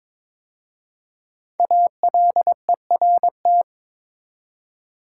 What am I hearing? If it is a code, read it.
Morse code, heard as a keyed tone, about 22 wpm, 700 Hz. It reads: ALERT
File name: 5 Letter Words - First 3 Letters Given - 22wpm